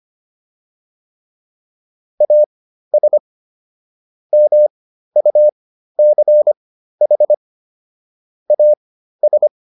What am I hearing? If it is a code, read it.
Morse code, heard as a keyed tone, about 25 wpm, spaced as for 13 wpm, 600 Hz. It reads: AS MUCH AS